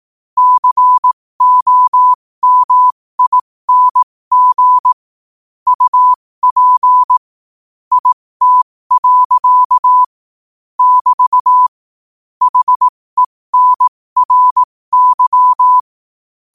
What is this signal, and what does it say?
Morse code, heard as a keyed tone, about 18 wpm, spaced as for 15 wpm, 1000 Hz. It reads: COMING UP IT. <BT> HENRY
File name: qotdfiles / 15wpm